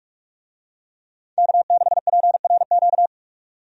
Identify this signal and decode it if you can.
Morse code, heard as a keyed tone, about 43 wpm, 700 Hz. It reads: K6PRQ
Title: USA Call Signs - 5 Characters - 45wpm